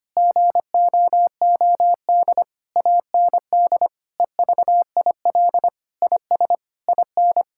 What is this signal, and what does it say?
Morse code, heard as a keyed tone, about 25 wpm, 700 Hz. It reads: GOOD AND EVIL IS IN
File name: practice / 25wpm / Sentences with 9-Word Phrases - 25wpm